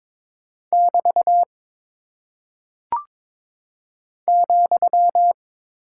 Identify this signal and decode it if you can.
Morse code, heard as a keyed tone, about 22 wpm, 700 Hz. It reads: <BT> ,
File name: practice / 22wpm / Single Punctuation-Prosign - 22wpm